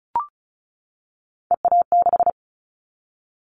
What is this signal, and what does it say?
Morse code, heard as a keyed tone, about 36 wpm, 700 Hz. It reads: EA6
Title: Callsign Prefixes for Top 100 DXCC - 35wpm